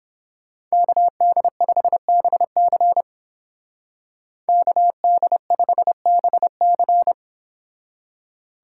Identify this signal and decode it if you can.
Morse code, heard as a keyed tone, about 32 wpm, 700 Hz. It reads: KD5BC KD5BC